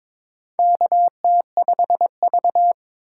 Morse code, heard as a keyed tone, about 22 wpm, 700 Hz. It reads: KT5V
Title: K1USN SST QSO Practice - 22wpm